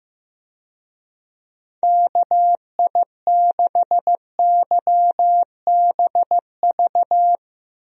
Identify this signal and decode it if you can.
Morse code, heard as a keyed tone, about 15 wpm, 700 Hz. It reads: KI6YBV
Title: USA Call Signs - 6 Characters - Repeated 3x - 15wpm